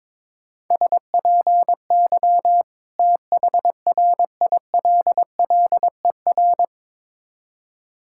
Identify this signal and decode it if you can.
Morse code, heard as a keyed tone, about 22 wpm, 690 Hz. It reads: SPY THRILLER
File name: Difficult Words with Hints - 22wpm